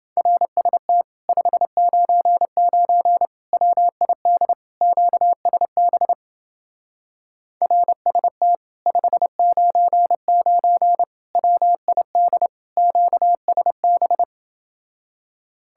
Morse code, heard as a keyed tone, about 29 wpm, 700 Hz. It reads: RST 599 WID QSB RST 599 WID QSB